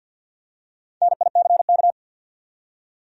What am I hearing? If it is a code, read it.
Morse code, heard as a keyed tone, about 53 wpm, 700 Hz. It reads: NICK
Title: USA Common Names - Max 5 Letters - 50wpm